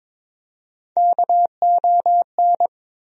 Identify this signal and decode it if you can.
Morse code, heard as a keyed tone, about 21 wpm, 700 Hz. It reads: KON
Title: USA Call Sign Postfixes - 3 Letters 22wpm